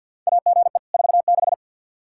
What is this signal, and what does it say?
Morse code, heard as a keyed tone, about 48 wpm, 700 Hz. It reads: AGE 46
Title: Practice QSO - Exchange 3 - More Info - 50wpm